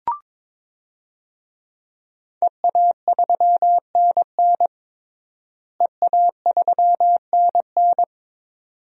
Morse code, heard as a keyed tone, about 22 wpm, 700 Hz. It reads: EA3NN EA3NN
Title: DXCC-100 Call Signs - 5 Characters - Repeated 2x - 22wpm